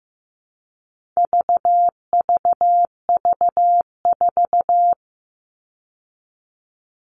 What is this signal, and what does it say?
Morse code, heard as a keyed tone, about 15 wpm, 700 Hz. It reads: VVV4
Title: All Cut Numbers - 4-Digit 15wpm